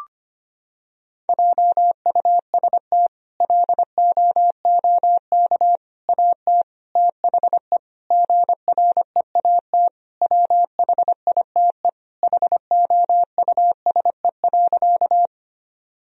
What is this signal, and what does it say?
Morse code, heard as a keyed tone, about 25 wpm, 700 Hz. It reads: JUST LOOK AT THE GREAT WHITE HOUSE.
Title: Sentences from Top 1000 Words - Review Part 1 - 25wpm